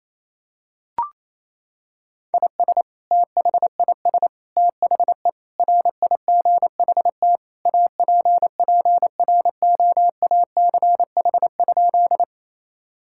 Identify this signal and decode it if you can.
Morse code, heard as a keyed tone, about 28 wpm, 700 Hz. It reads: IS THIS THE RIGHT APPROACH?
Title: Sentences from Top 1000 Words - Review Part 6 - 28wpm